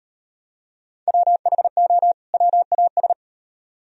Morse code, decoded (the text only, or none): WHO WAS